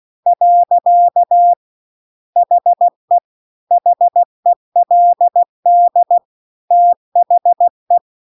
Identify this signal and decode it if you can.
Morse code, heard as a keyed tone, about 16 wpm, 700 Hz. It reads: . HE HELD THE
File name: CW-The-Big-Twoheaded-River-2-16WPM